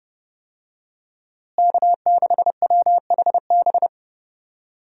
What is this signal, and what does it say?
Morse code, heard as a keyed tone, about 30 wpm, 700 Hz. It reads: K6WHB
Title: USA Call Signs - 5 Characters - Repeated 2x - 30wpm